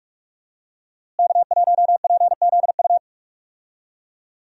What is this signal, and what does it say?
Morse code, heard as a keyed tone, about 46 wpm, 700 Hz. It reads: K1PZU
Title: USA Call Signs - 5 Characters - Repeated 2x - 45wpm